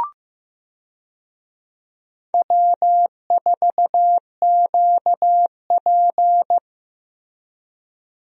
Morse code, heard as a keyed tone, about 15 wpm, 700 Hz. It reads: W4QP
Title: USA Call Signs - 4 Characters - Repeated 3x - 15wpm